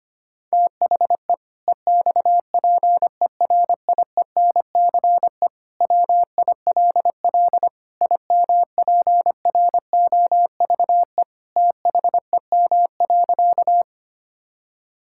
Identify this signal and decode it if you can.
Morse code, heard as a keyed tone, about 25 wpm, 700 Hz. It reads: THE EXPERIENCE WILL IMPROVE THEM.
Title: Sentences from Top 1000 Words - Review Part 5 - 25wpm